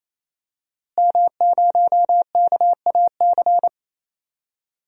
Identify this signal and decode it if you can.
Morse code, heard as a keyed tone, about 28 wpm, 700 Hz. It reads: M0KAC
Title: DXCC-100 Call Signs - 5 Characters - Repeated 3x - 28wpm